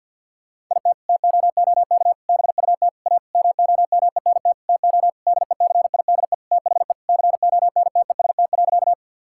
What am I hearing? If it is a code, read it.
Morse code, heard as a keyed tone, about 50 wpm, 700 Hz. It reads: IT TOOK BUT A MOMENT TO DECIDE THE CONTEST.
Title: Sentences from Top 1600 Words 50wpm